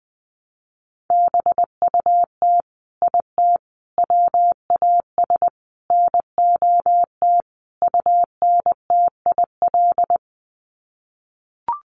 Morse code, heard as a keyed tone, about 20 wpm, 700 Hz. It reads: BUT IT WAS NOT UNTIL
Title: Sets of 5 Words Encore2 20wpm